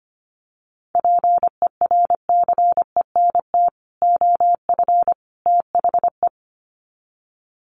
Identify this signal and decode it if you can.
Morse code, heard as a keyed tone, about 25 wpm, 700 Hz. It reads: PERCENT OF THE
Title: Sets of 3 Words 25wpm